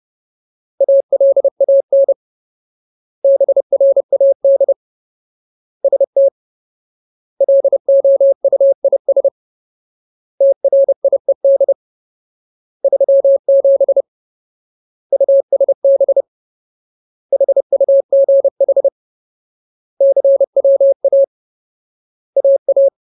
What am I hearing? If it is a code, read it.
Morse code, heard as a keyed tone, about 30 wpm, 550 Hz. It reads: ALAN BRAD ST LOUIS TRIED 37 USB HUGH CWA AA
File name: Ultimate Word List / 30WPM Ultimate